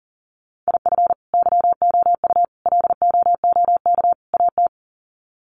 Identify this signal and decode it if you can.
Morse code, heard as a keyed tone, about 40 wpm, 700 Hz. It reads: IF YOU LOOK AT
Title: Sets of 4 Words 40wpm